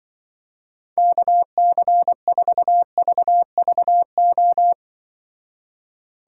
Morse code, heard as a keyed tone, about 25 wpm, 700 Hz. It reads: KC4VVO
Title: Speed Racing - USA 2x3 Call Signs 20wpm